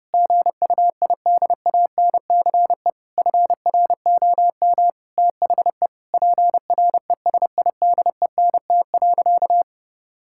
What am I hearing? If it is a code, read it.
Morse code, heard as a keyed tone, about 30 wpm, 700 Hz. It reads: GUIDANCE FROM THE PRESIDENT.
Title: Sentences from Top 1600 Words 30wpm